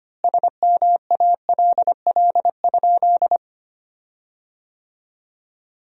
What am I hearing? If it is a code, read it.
Morse code, heard as a keyed tone, about 25 wpm, 700 Hz. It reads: SMALL?
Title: 4x Word-Spacing - Sentences from Top 200 Words - 25wpm